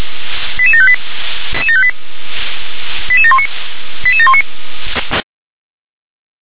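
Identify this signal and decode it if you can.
Five-tone selective call (ZVEI), heard as quick runs of five tones, tones of about 70 ms. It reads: 88758 88719 88719
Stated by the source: Five-tone ZVEI selective call received by SDR.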